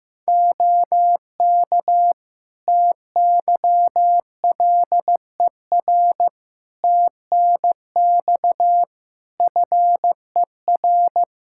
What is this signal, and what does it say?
Morse code, heard as a keyed tone, about 16 wpm, 700 Hz. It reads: OK TYLER TNX FER
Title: Practice QSO - Exchange 4 - Ending - 15wpm